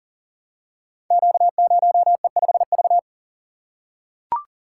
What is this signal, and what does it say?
Morse code, heard as a keyed tone, about 42 wpm, 700 Hz. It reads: Q0E5V
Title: Code Groups - 5 Letter:Numbers - 40wpm